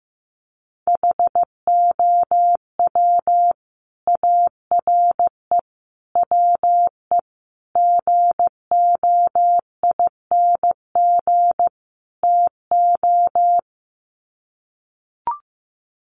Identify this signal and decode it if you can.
Morse code, heard as a keyed tone, about 15 wpm, 700 Hz. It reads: HOW ARE WE GOING TO
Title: Sets of 5 Words Encore 15wpm